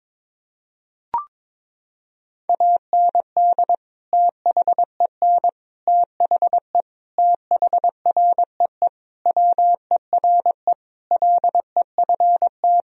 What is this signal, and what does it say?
Morse code, heard as a keyed tone, about 22 wpm, 700 Hz. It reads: AND THEN THE THREE WERE LEFT
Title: Sentences from Top 1000 Words - Review Part 1 - 22wpm